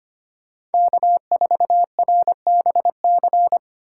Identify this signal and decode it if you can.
Morse code, heard as a keyed tone, about 32 wpm, 700 Hz. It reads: K4RBC